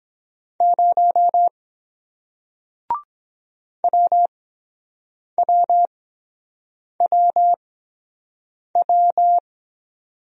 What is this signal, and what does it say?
Morse code, heard as a keyed tone, about 23 wpm, spaced as for 7 wpm, 700 Hz. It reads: 0 WWWW